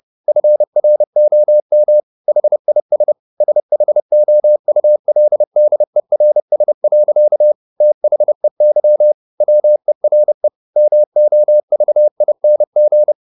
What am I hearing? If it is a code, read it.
Morse code, heard as a keyed tone, about 30 wpm, 600 Hz. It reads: FROM HIS SHOULDERS. THEY WERE MOVING
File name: the old man and the sea / the old man and the sea 1 30wpm